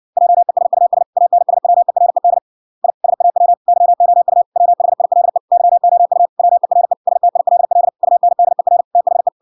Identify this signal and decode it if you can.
Morse code, heard as a keyed tone, about 59 wpm, 700 Hz. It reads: JESUS ANSWERED I SAW YOU WHILE YOU WERE STILL UNDER THE